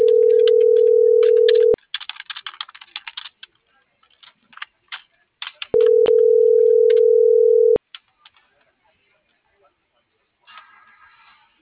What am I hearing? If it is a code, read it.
Ringback tone (North America), 440 + 480 Hz.